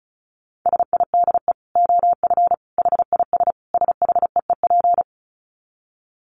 Morse code, heard as a keyed tone, about 34 wpm, 700 Hz. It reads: SIDE OF HIS SHEEP
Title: Dit Challenge 35wpm